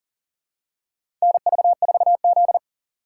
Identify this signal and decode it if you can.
Morse code, heard as a keyed tone, about 40 wpm, 700 Hz. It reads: NV4Z
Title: USA Call Signs - 4 Characters - 2x Character Spacing - 40wpm